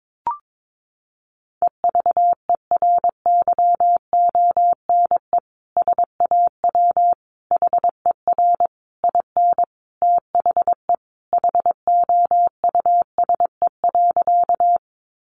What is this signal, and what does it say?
Morse code, heard as a keyed tone, about 22 wpm, 700 Hz. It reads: EVERYONE SAW HER IN THE HOUSE.
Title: Sentences from Top 800 Words 22wpm